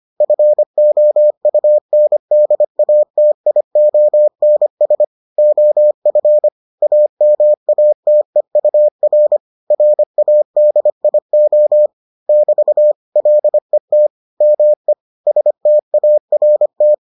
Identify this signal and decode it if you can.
Morse code, heard as a keyed tone, about 25 wpm, 600 Hz. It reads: FOUNDATIONS OF AMATEUR RADIO <BT> LET ME START